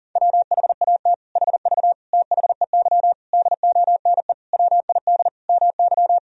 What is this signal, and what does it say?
Morse code, heard as a keyed tone, about 40 wpm, 700 Hz. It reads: WHAT HV THEY DONE WID MY
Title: Sentences from Top 500 Words and Abbreviations 40wpm